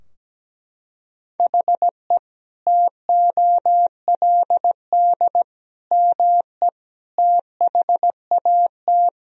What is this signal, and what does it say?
Morse code, heard as a keyed tone, about 17 wpm, 700 Hz. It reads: HE TOLD ME THAT
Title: Sentences from Top 1000 Words - Review Part 3 - 17wpm